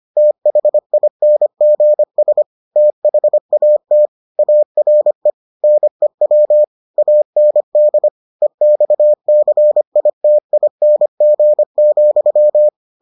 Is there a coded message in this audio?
Yes: Morse code from 0.2 to 12.7 s, heard as a keyed tone, about 25 wpm, 600 Hz. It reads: THINGS THAT ARE NEW AND EXCITING,